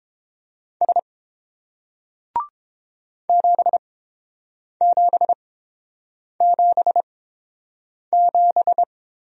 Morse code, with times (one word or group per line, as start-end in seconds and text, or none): 0.8-1.0 S
3.3-8.8 7777